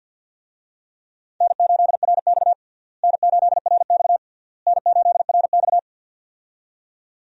Morse code, heard as a keyed tone, about 53 wpm, 700 Hz. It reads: N8RX N8RX N8RX
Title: USA Call Signs - 4 Characters - Repeated 3x - 50wpm